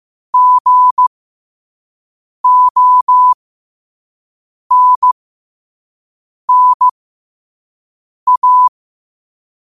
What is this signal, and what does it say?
Morse code, heard as a keyed tone, about 15 wpm, spaced as for 5 wpm, 1000 Hz. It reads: GONNA
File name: qotdfiles / 5wpm